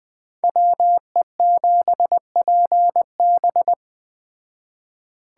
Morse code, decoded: WE7PB

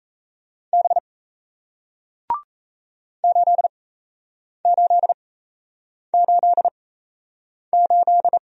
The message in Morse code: B 8888

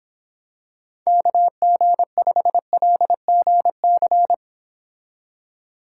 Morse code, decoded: KG5LGC

